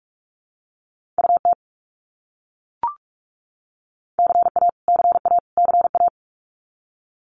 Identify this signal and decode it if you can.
Morse code, heard as a keyed tone, about 44 wpm, 700 Hz. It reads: UT CA CA CA
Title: US States Repeated 3x 45wpm